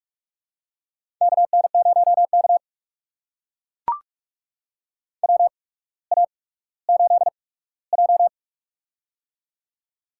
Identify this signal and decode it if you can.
Morse code, heard as a keyed tone, about 44 wpm, spaced as for 19 wpm, 700 Hz. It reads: KN0K WA8J
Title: USA Call Signs - 4 Characters - 8x Character Spacing - 45wpm